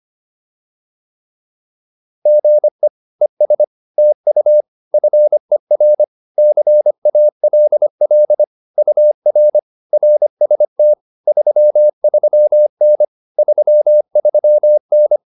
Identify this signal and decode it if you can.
Morse code, heard as a keyed tone, about 25 wpm, 600 Hz. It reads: GE ES TU FER CALL UR RST 33N 33N
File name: QSO105 25